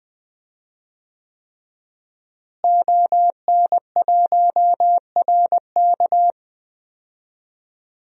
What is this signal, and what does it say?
Morse code, heard as a keyed tone, about 20 wpm, 700 Hz. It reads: ON1RK
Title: DXCC-100 Call Signs - 5 Characters - Repeated 3x - 20wpm